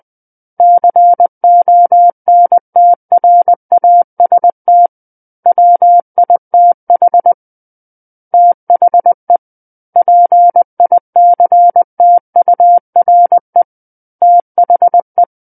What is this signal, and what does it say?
Morse code, heard as a keyed tone, about 20 wpm, 700 Hz. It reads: CONTRAST WITH THE PICTURE THE
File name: 20 WPM